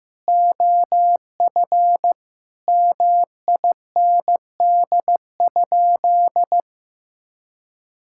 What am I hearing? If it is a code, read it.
Morse code, heard as a keyed tone, about 16 wpm, 700 Hz. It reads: OF MIND?
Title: Sentences from Top 1000 Words - Review Part 6 - 15wpm